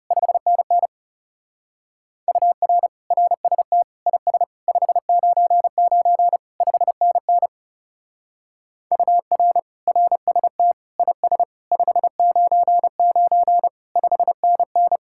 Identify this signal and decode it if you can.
Morse code, heard as a keyed tone, about 34 wpm, 700 Hz. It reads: 5NN UR RST IS 599 5NN UR RST IS 599 5NN
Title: Speed-Racing Common QSO Phrases Encore - 30wpm